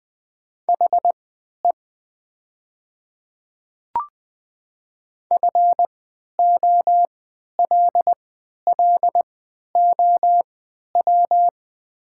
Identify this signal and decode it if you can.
Morse code, heard as a keyed tone, about 20 wpm, spaced as for 11 wpm, 700 Hz. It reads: HE FOLLOW
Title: 3x Character Spacing - Top 300 Words - 20wpm